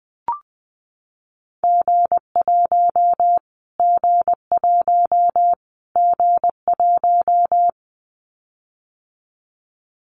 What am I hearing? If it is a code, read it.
Morse code, heard as a keyed tone, about 20 wpm, 700 Hz. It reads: G1 G1 G1
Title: Callsign Prefixes for Top 25 DXCC - Repeated 3x - 20wpm